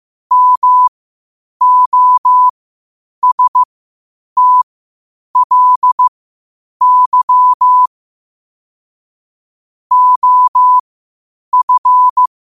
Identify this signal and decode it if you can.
Morse code, heard as a keyed tone, about 15 wpm, spaced as for 8 wpm, 1000 Hz. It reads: MOSTLY OF
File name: qotdfiles / 8wpm